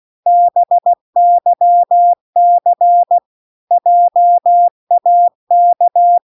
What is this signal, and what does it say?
Morse code, heard as a keyed tone, about 16 wpm, 700 Hz. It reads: BYC JAK